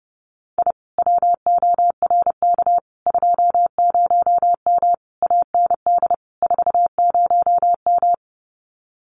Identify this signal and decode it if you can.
Morse code, heard as a keyed tone, about 30 wpm, 700 Hz. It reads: I WORK 20M AND 40M
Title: Common QSO Phrases Encore 30wpm